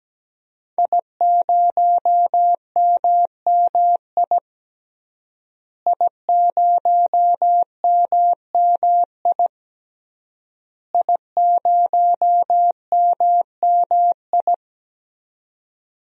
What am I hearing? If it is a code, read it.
Morse code, heard as a keyed tone, about 17 wpm, 700 Hz. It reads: I0MMI I0MMI I0MMI
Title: DXCC-100 Call Signs - 5 Characters - Repeated 3x - 17wpm